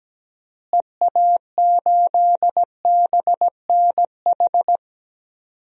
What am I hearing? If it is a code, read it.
Morse code, heard as a keyed tone, about 17 wpm, 700 Hz. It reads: EA8BNH
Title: DXCC-100 Call Signs - 6 Characters - Repeated 3x - 17wpm